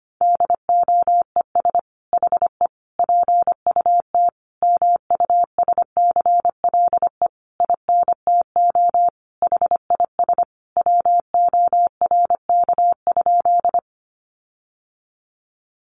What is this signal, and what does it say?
Morse code, heard as a keyed tone, about 25 wpm, 700 Hz. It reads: DOES HE PUT MUSCLE INTO HIS WORK?
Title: Sentences from Top 1300 Words 25wpm